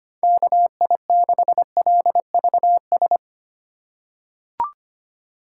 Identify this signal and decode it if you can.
Morse code, heard as a keyed tone, about 25 wpm, 700 Hz. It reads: KI6LVS
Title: USA Call Signs - 6 Characters - 4x Character Spacing - 25wpm